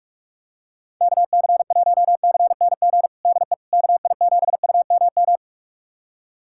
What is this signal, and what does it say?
Morse code, heard as a keyed tone, about 46 wpm, 700 Hz. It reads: KC1CNG DE KI7UMM
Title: USA Callsign DE Callsign 45wpm